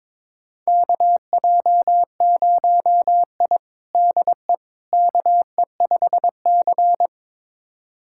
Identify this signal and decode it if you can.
Morse code, heard as a keyed tone, about 22 wpm, 700 Hz. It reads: KJ0I DE KE5C
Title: USA Callsign DE Callsign - 4 Characters - 22wpm